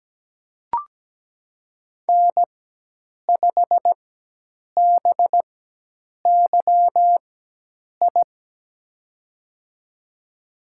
Morse code, heard as a keyed tone, about 17 wpm, spaced as for 8 wpm, 700 Hz. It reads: N5BYI
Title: USA Call Signs - 5 Characters - 4x Character Spacing - 17wpm